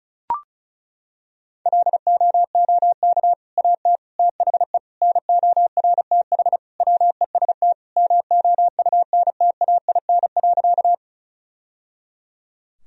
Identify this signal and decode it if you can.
Morse code, heard as a keyed tone, about 34 wpm, 700 Hz. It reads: LOOK AT THE NORTH WEST MOUNTAIN.
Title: Sentences from Top 1000 Words - Review Part 2 - 35wpm